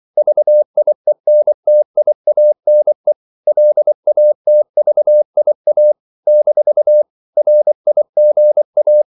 Morse code, heard as a keyed tone, about 24 wpm, 600 Hz. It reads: VIENTIANE LATVIA - RIGA
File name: CW / Practice / NationalCapitals24